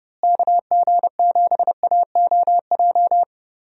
Morse code, heard as a keyed tone, about 32 wpm, 700 Hz. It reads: KG7AOJ
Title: USA Call Signs - 6 Characters - Repeated 3x - 30wpm